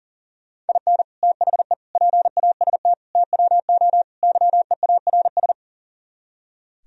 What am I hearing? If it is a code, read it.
Morse code, heard as a keyed tone, about 37 wpm, 700 Hz. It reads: IN THE PAST TWO YEARS
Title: Speed-Racing - Sets of 5 Words 40wpm